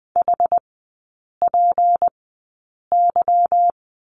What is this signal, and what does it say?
Morse code, heard as a keyed tone, about 20 wpm, spaced as for 8 wpm, 700 Hz. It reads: HPY
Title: 172 Lesson 28e - Number 9 - Three Characters 20wpm